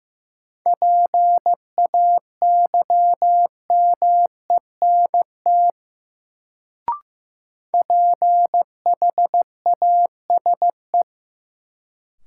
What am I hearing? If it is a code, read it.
Morse code, heard as a keyed tone, about 15 wpm, 700 Hz. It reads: PAYMENT PHASE
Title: Sentences from Top 1200 Words - New Word Drill Alt - 15wpm